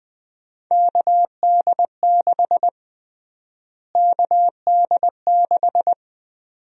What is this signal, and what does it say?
Morse code, heard as a keyed tone, about 20 wpm, 700 Hz. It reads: KD6 KD6
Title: USA Call Sign Prefixes - Repeated 2x - 20wpm